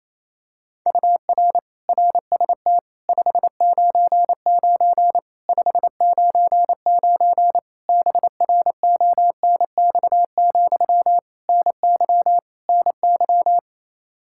Morse code, heard as a keyed tone, about 28 wpm, 700 Hz. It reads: UR RST 599 599 BRONX, NY NY
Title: Practice QSO - Exchange 1 - Beginning - 28wpm